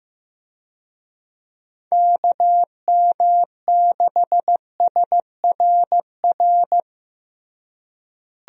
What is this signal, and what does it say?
Morse code, heard as a keyed tone, about 15 wpm, 700 Hz. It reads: KM6SRR
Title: USA Call Signs - 6 Characters - Repeated 3x - 15wpm